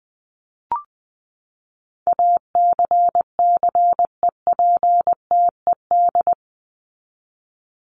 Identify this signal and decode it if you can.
Morse code, heard as a keyed tone, about 20 wpm, 700 Hz. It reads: ACCEPTED
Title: ED Suffix - Top 100 Words with Base Words 20wpm